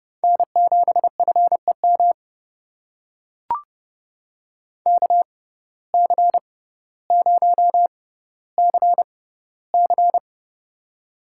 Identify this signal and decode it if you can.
Morse code, heard as a keyed tone, about 30 wpm, spaced as for 15 wpm, 700 Hz. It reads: N7FEM KC0CC
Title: USA Call Signs - 5 Characters - 6x Character Spacing - 30wpm